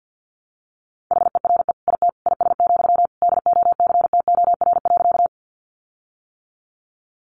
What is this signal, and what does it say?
Morse code, heard as a keyed tone, about 50 wpm, 700 Hz. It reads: HERE IT IS, DOCTOR.
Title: Sentences from Top 1500 Words 50wpm